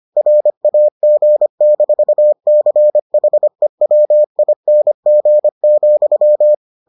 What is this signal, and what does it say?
Morse code, heard as a keyed tone, about 25 wpm, 600 Hz. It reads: RAG-CHEWING,